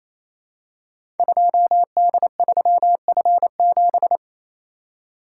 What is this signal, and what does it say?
Morse code, heard as a keyed tone, about 27 wpm, 700 Hz. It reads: 2D3F7
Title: Code Groups - 5 Letter:Numbers - 28wpm